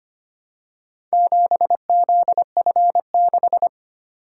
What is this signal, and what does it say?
Morse code, heard as a keyed tone, about 25 wpm, 700 Hz. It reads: 7ZF6
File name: practice / 25wpm / Four Letter-Number Combinations - 25wpm